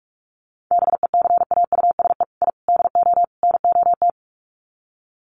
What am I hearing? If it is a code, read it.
Morse code, heard as a keyed tone, about 44 wpm, 700 Hz. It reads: BECAUSE I DO NOT
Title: Sets of 4 Words 45wpm